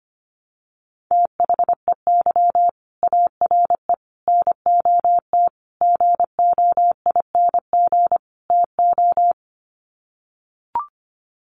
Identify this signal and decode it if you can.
Morse code, heard as a keyed tone, about 25 wpm, 700 Hz. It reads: THEY ARE NOT GOING TO
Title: Sets of 5 Words Encore2 25wpm